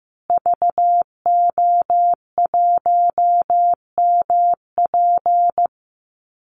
Morse code, heard as a keyed tone, about 15 wpm, 700 Hz. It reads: VO1MP
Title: Call Signs Repeated 2x - 15wpm